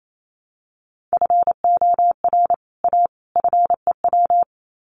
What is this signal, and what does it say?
Morse code, heard as a keyed tone, about 28 wpm, 700 Hz. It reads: FOR A FEW
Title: Sets of 3 Words 28wpm